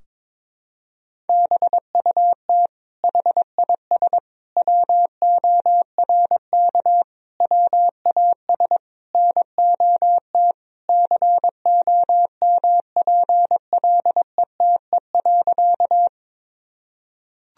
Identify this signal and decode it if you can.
Morse code, heard as a keyed tone, about 22 wpm, 700 Hz. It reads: BUT HIS WORK WAS NOT COMPLETE.
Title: Sentences from Top 1000 Words - Review Part 2 - 22wpm